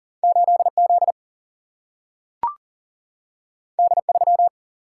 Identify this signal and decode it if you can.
Morse code, heard as a keyed tone, about 40 wpm, 700 Hz. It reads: 8Z D3
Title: Single-Letter Single-Number 40wpm